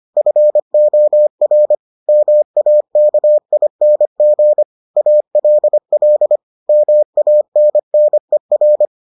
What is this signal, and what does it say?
Morse code, heard as a keyed tone, about 25 wpm, 600 Hz. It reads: FOR MAKING ALL MANNER